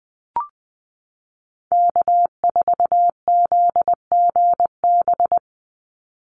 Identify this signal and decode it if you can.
Morse code, heard as a keyed tone, about 20 wpm, 700 Hz. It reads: K4ZGB
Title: Call Signs Repeated 2x - 20wpm